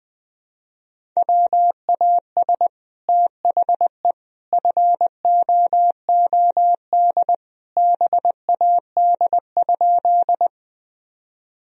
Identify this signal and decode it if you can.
Morse code, heard as a keyed tone, about 21 wpm, 700 Hz. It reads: WAS THE FOOD BAD?
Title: Sentences from Top 1000 Words - Review Part 1 - 20wpm